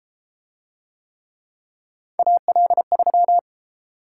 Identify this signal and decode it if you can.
Morse code, heard as a keyed tone, about 33 wpm, 700 Hz. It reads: AL3